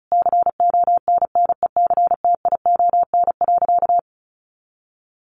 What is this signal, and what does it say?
Morse code, heard as a keyed tone, about 34 wpm, 700 Hz. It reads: CONNECTION.
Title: Sentences from Top 1300 Words 35wpm